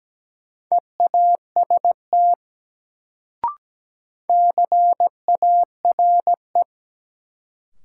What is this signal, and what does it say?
Morse code, heard as a keyed tone, about 17 wpm, 700 Hz. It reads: EAST CARE
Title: Sets of 3 Words - New Word Drill - 17wpm